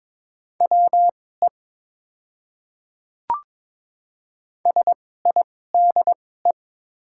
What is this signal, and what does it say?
Morse code, heard as a keyed tone, about 22 wpm, spaced as for 16 wpm, 700 Hz. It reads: WE SIDE
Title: 2x Character Spacing - Top 200 Words - 22wpm